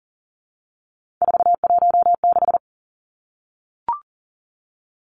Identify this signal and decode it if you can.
Morse code, heard as a keyed tone, about 40 wpm, 700 Hz. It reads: 416
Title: Numbers - 3-Digit 40wpm